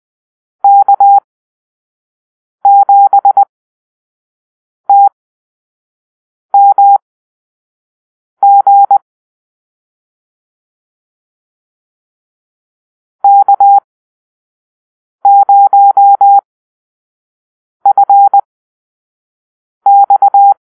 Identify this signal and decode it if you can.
Morse code, heard as a keyed tone, about 20 wpm, spaced as for 5 wpm, 800 Hz. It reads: K7TMG K0FX
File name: Call-Sign Drill